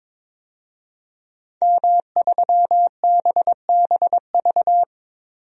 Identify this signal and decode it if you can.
Morse code, heard as a keyed tone, about 22 wpm, 700 Hz. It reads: M3BBV
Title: DXCC-100 Call Signs - 5 Characters - 22wpm